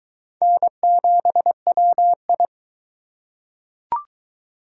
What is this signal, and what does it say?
Morse code, heard as a keyed tone, about 20 wpm, 700 Hz. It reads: N7WI